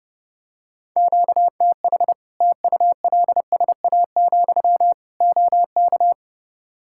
Morse code, heard as a keyed tone, about 32 wpm, 700 Hz. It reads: QTH TULSA, OK
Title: Practice QSO - Exchange 1 - Beginning - 30wpm